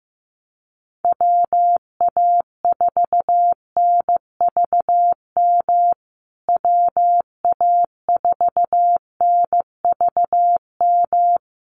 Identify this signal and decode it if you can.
Morse code, heard as a keyed tone, about 15 wpm, 700 Hz. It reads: WA4NVM WA4NVM
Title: USA Call Signs Repeated 2x 15wpm